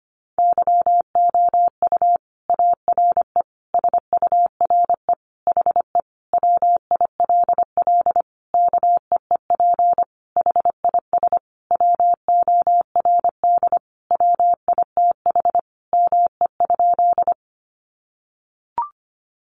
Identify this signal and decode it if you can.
Morse code, heard as a keyed tone, about 25 wpm, 700 Hz. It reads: YOU ARE SURE HE WILL KEEP HIS WORD WITH ME?
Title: Sentences from Top 300 Words 25wpm